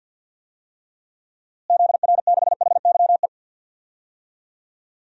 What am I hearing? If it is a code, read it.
Morse code, heard as a keyed tone, about 53 wpm, 700 Hz. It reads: ZR6HYE
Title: DXCC-100 Call Signs - 6 Characters - Repeated 2x - 50wpm